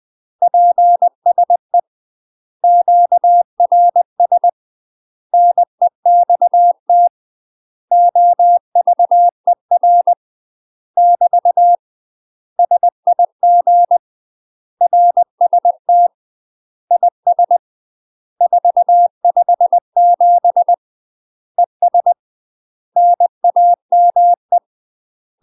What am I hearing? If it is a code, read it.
Morse code, heard as a keyed tone, about 20 wpm, 700 Hz. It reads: PSE QRS NEXT OVER <BT> SIG RST IS 457 ES NAME
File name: UK MASTER QSOs TRAINING / 20WPM / UK MASTER QSOs 0004